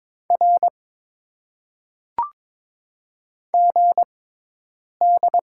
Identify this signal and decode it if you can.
Morse code, heard as a keyed tone, about 22 wpm, spaced as for 8 wpm, 700 Hz. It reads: R GD
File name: Top 90 QSO Elements - 6x Character Spacing - 22wpm